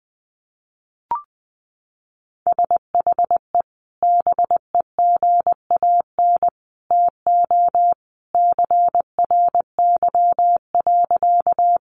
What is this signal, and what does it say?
Morse code, heard as a keyed tone, about 20 wpm, 700 Hz. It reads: SHE BEGAN TO CRY.
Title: Sentences from Top 500 Words 20wpm